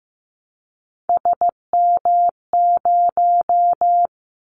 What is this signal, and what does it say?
Morse code, heard as a keyed tone, about 15 wpm, 700 Hz. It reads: SM0
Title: Callsign Prefixes for Top 25 DXCC - 15wpm